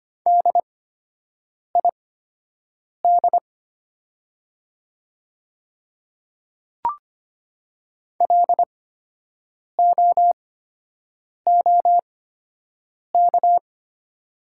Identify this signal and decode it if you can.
Morse code, heard as a keyed tone, about 25 wpm, spaced as for 7 wpm, 700 Hz. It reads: DID LOOK
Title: 8x Character Spacing - Top 300 Words - 25wpm